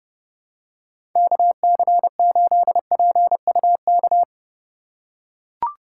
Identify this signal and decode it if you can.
Morse code, heard as a keyed tone, about 31 wpm, 700 Hz. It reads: KC8PUK
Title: USA Call Signs - 6 Characters - 8x Character Spacing - 30wpm